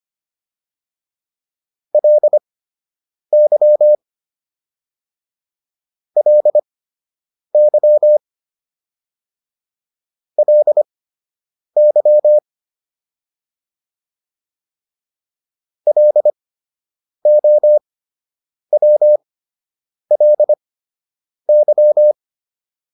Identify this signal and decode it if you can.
Morse code, heard as a keyed tone, about 25 wpm, spaced as for 8 wpm, 600 Hz. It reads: LY LY LY LOWLY